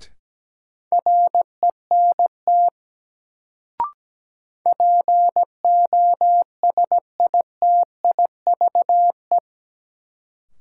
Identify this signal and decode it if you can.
Morse code, heard as a keyed tone, about 17 wpm, 700 Hz. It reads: RENT POSITIVE
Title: Sentences from Top 1000 Words - Review Part 5 - New Word Drill - 17wpm